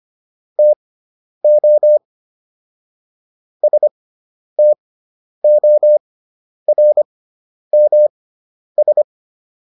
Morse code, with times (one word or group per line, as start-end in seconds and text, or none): 0.6-2.0 TO
3.6-9.0 STORMS